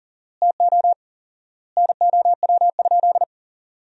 Morse code, heard as a keyed tone, about 40 wpm, 700 Hz. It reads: TO NOW?
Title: 4x Word-Spacing - Sentences from Top 100 Words - 40wpm